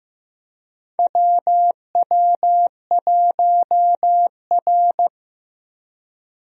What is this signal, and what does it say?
Morse code, heard as a keyed tone, about 15 wpm, 700 Hz. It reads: WW1R
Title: USA Call Signs - 4 Characters - Repeated 2x - 15wpm